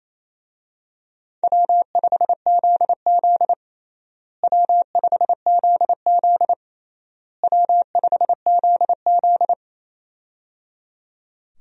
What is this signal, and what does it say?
Morse code, heard as a keyed tone, about 28 wpm, 700 Hz. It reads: W5ZZ W5ZZ W5ZZ
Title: USA Call Signs - 4 Characters - Repeated 3x - 28wpm